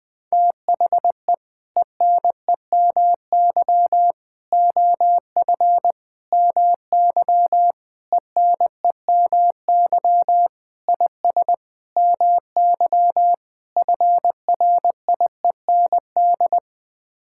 Morse code, heard as a keyed tone, about 19 wpm, 700 Hz. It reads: THE ENEMY OF MY ENEMY IS MY FRIEND
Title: Sets of 8 Words - 20wpm